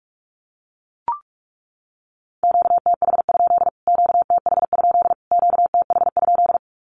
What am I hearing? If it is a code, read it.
Morse code, heard as a keyed tone, about 44 wpm, 700 Hz. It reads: QTH? QTH? QTH?
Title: Q-Codes Repeated 3x 45wpm